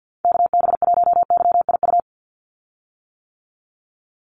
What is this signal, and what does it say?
Morse code, heard as a keyed tone, about 50 wpm, 700 Hz. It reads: KB1YIU
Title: USA Call Signs 50wpm